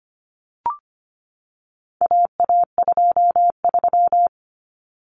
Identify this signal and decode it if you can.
Morse code, heard as a keyed tone, about 25 wpm, 700 Hz. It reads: AA23
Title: Common Cut Numbers - 4-Digit 25wpm